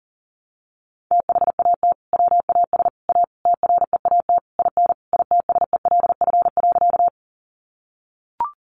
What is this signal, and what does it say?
Morse code, heard as a keyed tone, about 40 wpm, 700 Hz. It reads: THAT WAS A TREAT IN ITSELF.
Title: Sentences from Top 1300 Words 40wpm